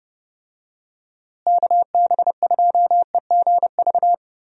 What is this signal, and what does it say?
Morse code, heard as a keyed tone, about 30 wpm, 700 Hz. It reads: KB2EGV
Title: USA Call Signs - 6 Characters - Repeated 3x - 30wpm